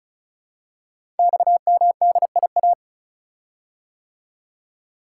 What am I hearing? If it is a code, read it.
Morse code, heard as a keyed tone, about 36 wpm, 700 Hz. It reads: XMDIA
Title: Code Groups - 5 Letters - 35wpm